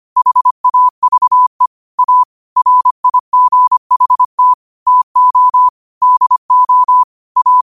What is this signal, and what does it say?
Morse code, heard as a keyed tone, about 25 wpm, 1000 Hz. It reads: HAVE A RIGHT TO DO A